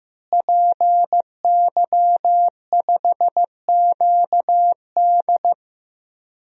Morse code, heard as a keyed tone, about 15 wpm, 700 Hz. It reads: PY5QD
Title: DXCC-100 Call Signs - 5 Characters - Repeated 3x - 15wpm